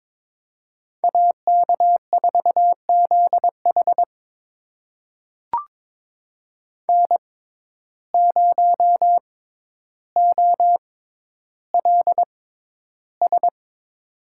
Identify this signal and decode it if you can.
Morse code, heard as a keyed tone, about 22 wpm, spaced as for 11 wpm, 700 Hz. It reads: AK4ZH N0OLS
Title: USA Call Signs - 5 Characters - 6x Character Spacing - 22wpm